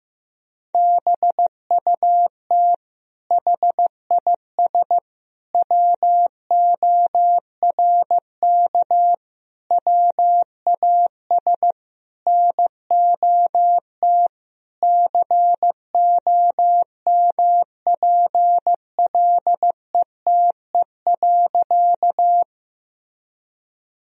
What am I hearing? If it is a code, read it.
Morse code, heard as a keyed tone, about 15 wpm, 700 Hz. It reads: BUT HIS WORK WAS NOT COMPLETE.
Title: Sentences from Top 1000 Words - Review Part 2 - 15wpm